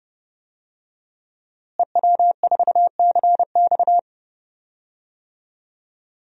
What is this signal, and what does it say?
Morse code, heard as a keyed tone, about 30 wpm, 700 Hz. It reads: EW4CX
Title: DXCC-100 Call Signs - 5 Characters - Repeated 2x - 30wpm